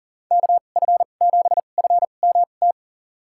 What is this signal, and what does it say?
Morse code, heard as a keyed tone, about 40 wpm, spaced as for 29 wpm, 700 Hz. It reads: KF7FMT
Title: USA Call Signs - 6 Characters - 2x Character Spacing - 40wpm